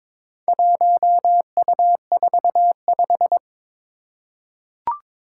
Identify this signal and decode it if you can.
Morse code, heard as a keyed tone, about 22 wpm, 700 Hz. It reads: 1U45
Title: Four Challenging Letter-Number Combinations 22wpm